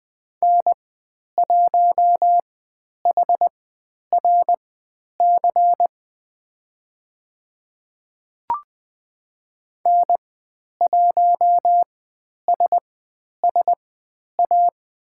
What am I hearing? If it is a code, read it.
Morse code, heard as a keyed tone, about 20 wpm, spaced as for 10 wpm, 700 Hz. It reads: N1HRC N1SSA
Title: 095 Lesson 16g - Letter c - Call Signs 20wpm